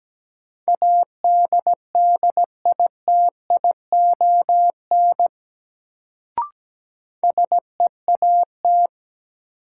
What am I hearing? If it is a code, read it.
Morse code, heard as a keyed tone, about 17 wpm, 700 Hz. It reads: ADDITION SEAT
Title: Sentences from Top 1500 Words - New Word Drill Alt - 17wpm